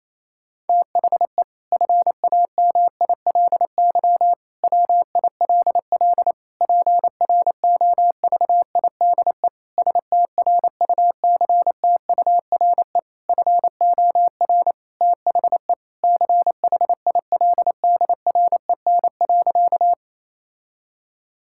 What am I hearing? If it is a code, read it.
Morse code, heard as a keyed tone, about 28 wpm, 700 Hz. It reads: THE FAMILY WILL PROVIDE STRUCTURE FOR THE CHILDREN.
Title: Sentences from Top 1000 Words - Review Part 6 - 28wpm